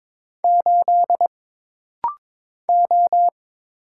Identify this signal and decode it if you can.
Morse code, heard as a keyed tone, about 22 wpm, 700 Hz. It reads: 8 O
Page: This rapid-fire timing is geared to push your ICR (Instant Character Recognition) skills.